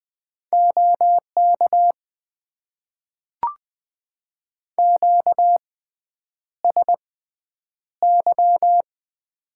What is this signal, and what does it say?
Morse code, heard as a keyed tone, about 20 wpm, spaced as for 9 wpm, 700 Hz. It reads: OK QSY
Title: Top 90 QSO Elements - 6x Character Spacing - 20wpm